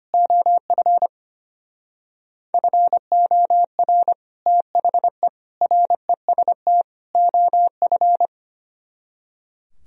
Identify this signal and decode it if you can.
Morse code, heard as a keyed tone, about 31 wpm, 700 Hz. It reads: OF FOR THE REST OF
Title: Speed-Racing - Sets of 4 Words 25wpm